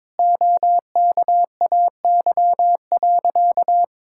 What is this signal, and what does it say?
Morse code, heard as a keyed tone, about 21 wpm, 700 Hz. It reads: OKAY.
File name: practice / 22wpm / Sentences with 2-Word Phrases - 22wpm